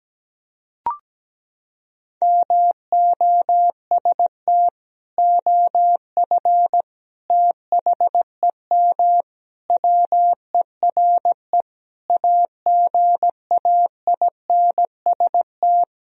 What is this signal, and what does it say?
Morse code, heard as a keyed tone, about 17 wpm, 700 Hz. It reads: MOST OF THEM WERE AGAINST
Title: Sentences from Top 1000 Words - Review Part 1 - 17wpm